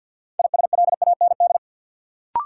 Morse code, heard as a keyed tone, about 53 wpm, 700 Hz. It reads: ISLAND
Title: Sentences from Top 500 Words - New Word Drill - 50wpm